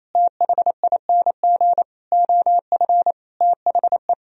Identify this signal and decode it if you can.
Morse code, heard as a keyed tone, about 28 wpm, 700 Hz. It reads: THING OF THE
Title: Sentences from Top 1000 Words - Review Part 3 - 28wpm